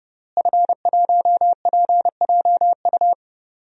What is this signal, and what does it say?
Morse code, heard as a keyed tone, about 32 wpm, 700 Hz. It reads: F1PJU
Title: DXCC-100 Call Signs - 5 Characters - Repeated 2x - 30wpm